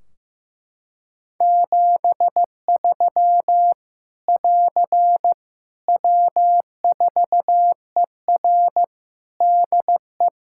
Morse code, heard as a keyed tone, about 15 wpm, 700 Hz. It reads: 73 <AR> W4ER DE
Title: Practice QSO - Exchange 4 - Ending - 15wpm